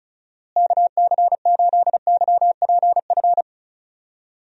Morse code, heard as a keyed tone, about 36 wpm, 700 Hz. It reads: KC8YPF
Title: USA Call Signs - 6 Characters - Repeated 3x - 35wpm